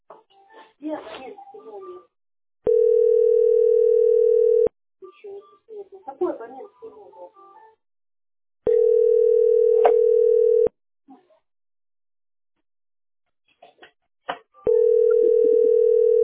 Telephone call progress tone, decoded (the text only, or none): ringback tone